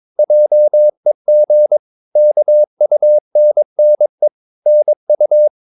Morse code, heard as a keyed tone, about 22 wpm, 600 Hz. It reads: JEG KUNNE NU
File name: Ferie QRS 22wpm